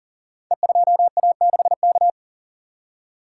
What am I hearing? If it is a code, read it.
Morse code, heard as a keyed tone, about 38 wpm, 700 Hz. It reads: E2A6K
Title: Code Groups - 5 Letter:Numbers - 40wpm